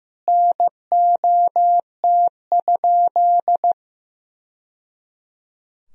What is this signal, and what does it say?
Morse code, heard as a keyed tone, about 15 wpm, 700 Hz. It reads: NOT?
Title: Sentences from Top 1000 Words - Review Part 3 - 15wpm